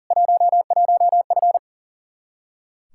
Morse code, heard as a keyed tone, about 40 wpm, 700 Hz. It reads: 11F
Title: Practice QSO - Exchange 2 - Info - 40wpm